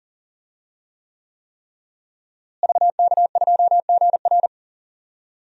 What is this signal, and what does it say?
Morse code, heard as a keyed tone, about 37 wpm, 700 Hz. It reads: VK2GR
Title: DXCC-100 Call Signs - 5 Characters - Repeated 3x - 40wpm